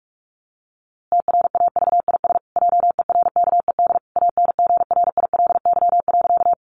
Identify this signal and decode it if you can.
Morse code, heard as a keyed tone, about 44 wpm, 700 Hz. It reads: TRAVIS JERKED ANGRILY.
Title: A Sound Of Thunder 45wpm